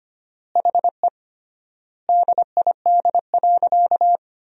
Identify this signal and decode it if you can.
Morse code, heard as a keyed tone, about 25 wpm, 700 Hz. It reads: HE DID.
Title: 3x Word-Spacing - Sentences from Top 500 Words - 25wpm